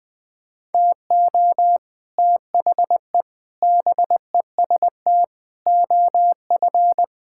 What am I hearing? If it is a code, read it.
Morse code, heard as a keyed tone, about 20 wpm, 700 Hz. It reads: TO THE BEST OF
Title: Sets of 9 Words - 20wpm